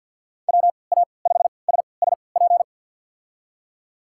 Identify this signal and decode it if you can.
Morse code, heard as a keyed tone, about 47 wpm, spaced as for 28 wpm, 700 Hz. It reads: WA5SSP
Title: USA Call Signs - 6 Characters - 3x Character Spacing - 50wpm